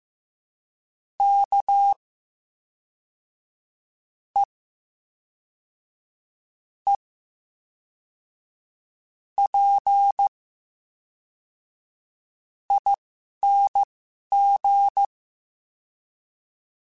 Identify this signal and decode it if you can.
Morse code, heard as a keyed tone, about 15 wpm, spaced as for 11 wpm, 800 Hz. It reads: K E E P ING
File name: Special INGs / Special INGs-mp3s / VerySlowToFaster Ing @10